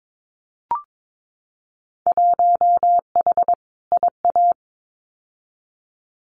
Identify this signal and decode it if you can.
Morse code, heard as a keyed tone, about 22 wpm, 700 Hz. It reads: 1H IA
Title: Winter Field Day Exchanges 22wpm